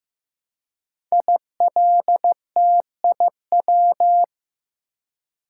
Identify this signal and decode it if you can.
Morse code, heard as a keyed tone, about 15 wpm, 700 Hz. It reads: ILTIW
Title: Code Groups - 5 Letters - 15wpm